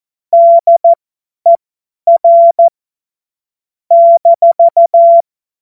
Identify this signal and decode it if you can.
Morse code, heard as a keyed tone, about 14 wpm, spaced as for 10 wpm, 680 Hz. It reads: DER -